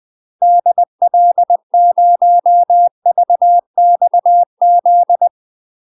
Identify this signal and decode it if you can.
Morse code, heard as a keyed tone, about 20 wpm, 700 Hz. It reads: DL0VXZ
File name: UK MASTER QSOs TRAINING / 20WPM / UK MASTER QSOs 0055